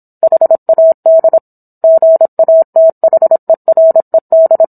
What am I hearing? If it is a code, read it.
Morse code, heard as a keyed tone, about 27 wpm, 650 Hz. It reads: HAD GATHERED